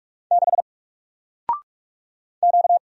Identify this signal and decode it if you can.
Morse code, heard as a keyed tone, about 46 wpm, 700 Hz. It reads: 6 Q
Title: Single Character (Letter:Number:Punctuation:Prosign) Alt 45wpm